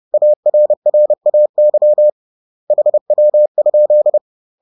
Morse code, heard as a keyed tone, about 30 wpm, 600 Hz. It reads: ARRAY HW?
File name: 30wpm-7